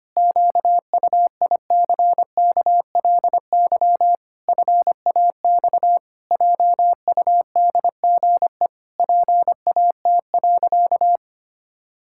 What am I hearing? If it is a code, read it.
Morse code, heard as a keyed tone, about 25 wpm, 700 Hz. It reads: QUICKLY FAX JUDGE PAT.
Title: Pangram Challenge v2 - 25wpm